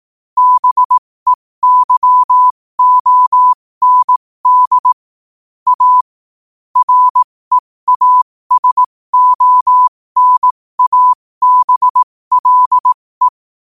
Morse code, heard as a keyed tone, about 18 wpm, spaced as for 15 wpm, 1000 Hz. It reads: BEYOND A REASONABLE
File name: qotdfiles / 15wpm